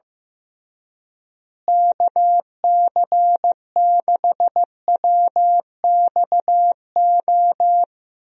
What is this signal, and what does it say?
Morse code, heard as a keyed tone, about 15 wpm, 700 Hz. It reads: KC6WXO
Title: USA Call Signs - 6 Characters - Repeated 2x - 15wpm